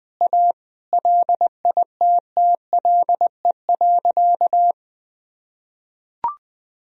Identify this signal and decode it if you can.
Morse code, heard as a keyed tone, about 20 wpm, 700 Hz. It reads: A LITTLE.
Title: Sentences from Top 1000 Words - Review Part 2 - 20wpm